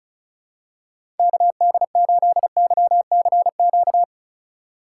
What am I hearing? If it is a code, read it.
Morse code, heard as a keyed tone, about 36 wpm, 700 Hz. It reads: KD8YCQ
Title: USA Call Signs - 2x Character Spacing - 35wpm